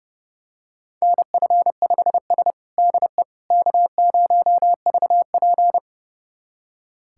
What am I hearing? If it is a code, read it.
Morse code, heard as a keyed tone, about 32 wpm, 700 Hz. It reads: NF5S DE K0VP
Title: USA Callsign DE Callsign - 4 Characters - 30wpm